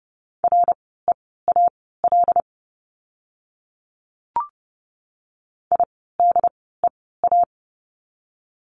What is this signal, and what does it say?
Morse code, heard as a keyed tone, about 30 wpm, spaced as for 17 wpm, 700 Hz. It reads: REAL IDEA
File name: course / 30:10wpm / 108 Lesson 18f - Letter m - Words 30wpm